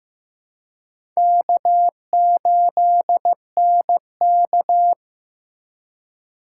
Morse code, heard as a keyed tone, about 15 wpm, 700 Hz. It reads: K8NK
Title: USA Call Signs - 4 Characters - 2x Character Spacing - 15wpm